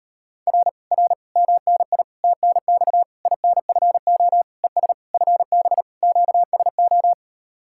Morse code, heard as a keyed tone, about 38 wpm, 700 Hz. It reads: R R MNI TNX INFO ES FB QSO